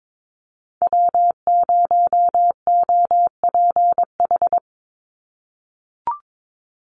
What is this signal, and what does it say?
Morse code, heard as a keyed tone, about 22 wpm, 700 Hz. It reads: W0OPH
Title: USA Call Signs 22wpm